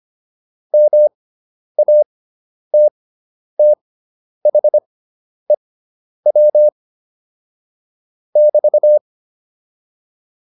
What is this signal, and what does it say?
Morse code, heard as a keyed tone, about 25 wpm, spaced as for 10 wpm, 600 Hz. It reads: MATTHEW <BT>
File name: QSO303 10